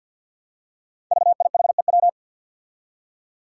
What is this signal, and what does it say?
Morse code, heard as a keyed tone, about 46 wpm, 700 Hz. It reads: VIHEW
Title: Code Groups - 5 Letter:Numbers - 50wpm